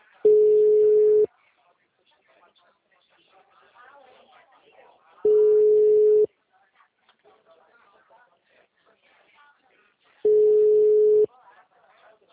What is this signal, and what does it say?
Ringback tone (Europe, 425 Hz), with speech.
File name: out- -2007- - - .85764